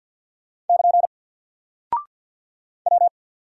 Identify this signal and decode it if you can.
Morse code, heard as a keyed tone, about 50 wpm, 700 Hz. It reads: <KN> W
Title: Single Character (Letter:Number:Punctuation:Prosign) Alt 50wpm